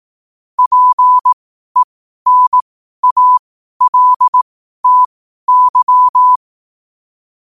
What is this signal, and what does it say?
Morse code, heard as a keyed tone, about 18 wpm, spaced as for 12 wpm, 1000 Hz. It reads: PENALTY
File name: qotdfiles / 12wpm